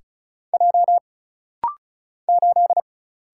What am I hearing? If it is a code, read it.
Morse code, heard as a keyed tone, about 35 wpm, 700 Hz. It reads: J 8
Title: Single Letter-Number - Rapid-Fire 35wpm